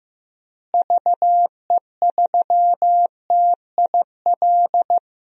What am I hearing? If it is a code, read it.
Morse code, heard as a keyed tone, about 15 wpm, 700 Hz. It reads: VE3TIL
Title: DXCC-100 Call Signs - 6 Characters - Repeated 3x - 15wpm